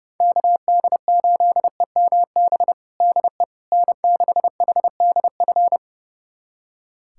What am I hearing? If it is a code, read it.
Morse code, heard as a keyed tone, about 32 wpm, 700 Hz. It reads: KD8EMB DE N6HDF
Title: USA Callsign DE Callsign 30wpm